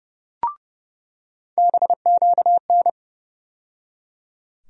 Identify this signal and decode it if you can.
Morse code, heard as a keyed tone, about 30 wpm, 700 Hz. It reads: BQN
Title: USA Call Sign Postfixes - 3 Letters 30wpm